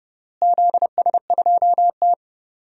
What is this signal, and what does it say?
Morse code, heard as a keyed tone, about 28 wpm, 700 Hz. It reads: ZS2T
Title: DXCC-100 Call Signs - 4 Characters - Repeated 3x - 30wpm